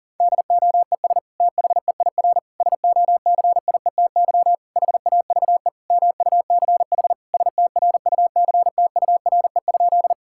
Morse code, heard as a keyed tone, about 40 wpm, 690 Hz. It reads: DOES THEIR SOCIETY HAVE MUCH STRUCTURE?
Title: Sentences from Top 1000 Words - Review Part 6 - 40wpm